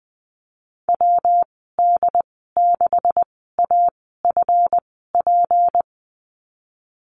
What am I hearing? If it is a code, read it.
Morse code, heard as a keyed tone, about 20 wpm, spaced as for 15 wpm, 700 Hz. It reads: WD6AFP
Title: 173 Lesson 28g - Number 9 - Call Signs 20wpm